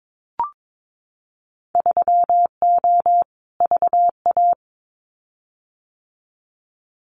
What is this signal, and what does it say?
Morse code, heard as a keyed tone, about 22 wpm, 700 Hz. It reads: 3O VA
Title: Winter Field Day Exchanges 22wpm